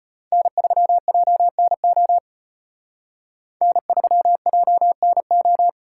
Morse code, heard as a keyed tone, about 32 wpm, 700 Hz. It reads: N3JNO N3JNO